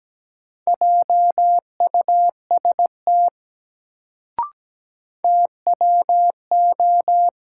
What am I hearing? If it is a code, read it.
Morse code, heard as a keyed tone, about 17 wpm, 700 Hz. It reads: JUST TWO
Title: Sets of 2 Words - New Word Drill - 17wpm